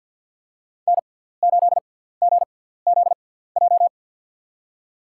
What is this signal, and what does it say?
Morse code, heard as a keyed tone, about 52 wpm, spaced as for 17 wpm, 700 Hz. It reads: N8GZJ
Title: USA Call Signs - 5 Characters - 6x Character Spacing - 50wpm